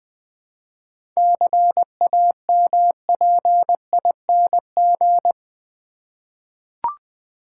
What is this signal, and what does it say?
Morse code, heard as a keyed tone, about 20 wpm, 700 Hz. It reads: CAMPING
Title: 7 Letter Words - First 3 Letters Given - 20wpm